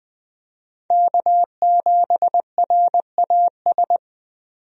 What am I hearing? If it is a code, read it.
Morse code, heard as a keyed tone, about 21 wpm, 700 Hz. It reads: K7RAS
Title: USA Call Signs - 5 Characters - 20wpm